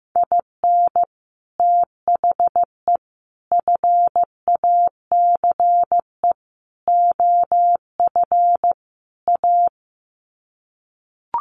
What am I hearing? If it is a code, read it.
Morse code, heard as a keyed tone, about 15 wpm, 700 Hz. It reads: IN THE FACE OF A
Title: Sets of 5 Words Encore 15wpm